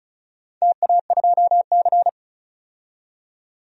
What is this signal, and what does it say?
Morse code, heard as a keyed tone, about 36 wpm, 700 Hz. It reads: TA2C
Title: DXCC-100 Call Signs - 4 Characters - 35wpm